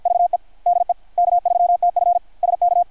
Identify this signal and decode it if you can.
Morse code, heard as a keyed tone, about 51 wpm, 700 Hz. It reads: VE DE K3TF SK
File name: archived / CRSnet